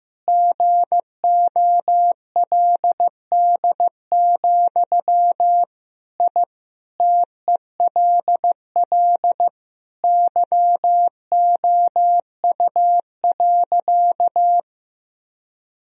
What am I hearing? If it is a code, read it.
Morse code, heard as a keyed tone, about 15 wpm, 700 Hz. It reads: GOLD, I TELL YOU.
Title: Sentences from Top 1000 Words - Review Part 3 - 15wpm